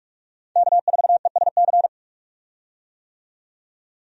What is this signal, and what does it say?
Morse code, heard as a keyed tone, about 43 wpm, 700 Hz. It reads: K4ESC
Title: USA Call Signs - 5 Characters - 45wpm